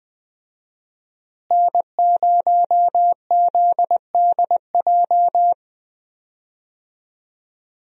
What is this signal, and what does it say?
Morse code, heard as a keyed tone, about 20 wpm, 700 Hz. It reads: N0ZDJ
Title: USA Call Signs - 5 Characters - Repeated 2x - 20wpm